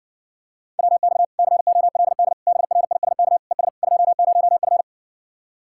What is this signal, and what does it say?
Morse code, heard as a keyed tone, about 70 wpm, 700 Hz. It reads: WX COLD BRISK ES 20F